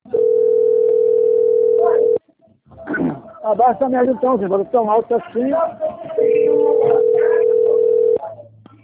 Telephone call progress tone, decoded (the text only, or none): ringback tone